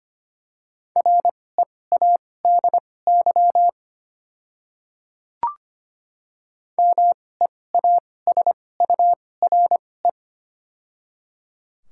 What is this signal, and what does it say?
Morse code, heard as a keyed tone, about 25 wpm, spaced as for 18 wpm, 700 Hz. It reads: READY MEASURE
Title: 2x Character Spacing - Top 500 Words - 25wpm